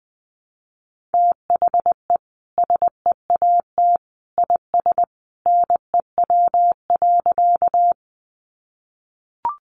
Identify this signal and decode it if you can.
Morse code, heard as a keyed tone, about 20 wpm, 700 Hz. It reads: THE SEAT IS NEW.
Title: Sentences from Top 1500 Words 20wpm